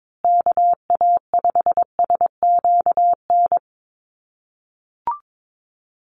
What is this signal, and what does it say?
Morse code, heard as a keyed tone, about 22 wpm, 700 Hz. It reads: KA5SQN
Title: USA Call Signs Repeated 3x 22wpm